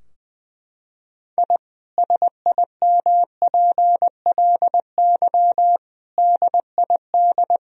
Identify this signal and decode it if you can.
Morse code, heard as a keyed tone, about 21 wpm, 700 Hz. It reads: I SIMPLY DID
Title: Sentences from Top 1000 Words - Review Part 4 - 20wpm